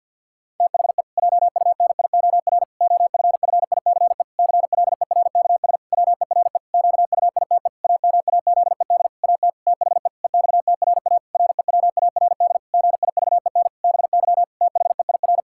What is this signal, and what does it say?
Morse code, heard as a keyed tone, about 50 wpm, 700 Hz. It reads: THE JUNIOR OFFICE CLERKS WERE QUITE AMAZED AT THE EXTRA REWARD GIVEN BY THEIR
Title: Pangram Challenge v2 - 50wpm